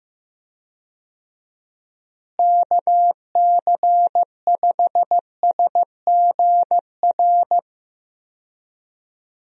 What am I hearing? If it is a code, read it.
Morse code, heard as a keyed tone, about 15 wpm, 700 Hz. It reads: KC5SGR
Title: USA Call Signs - 6 Characters - Repeated 3x - 15wpm